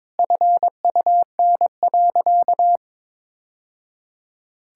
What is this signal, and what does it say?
Morse code, heard as a keyed tone, about 23 wpm, 700 Hz. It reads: FUN.
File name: practice / 22wpm / Sentences with 2-Word Phrases - 22wpm